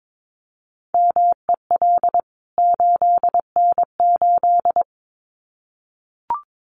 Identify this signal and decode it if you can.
Morse code, heard as a keyed tone, about 22 wpm, 700 Hz. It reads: MEL 8N8
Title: CWT Exchange 22wpm